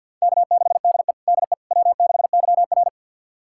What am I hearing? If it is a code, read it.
Morse code, heard as a keyed tone, about 52 wpm, 700 Hz. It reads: K6DE DE W6YR
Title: USA Callsign DE Callsign - 4 Characters - 50wpm